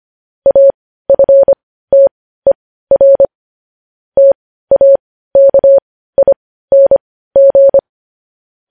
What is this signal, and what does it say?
Morse code, heard as a keyed tone, about 25 wpm, spaced as for 15 wpm, 550 Hz. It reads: AFTER TAKING
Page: news headlines in morse code at 15 wpm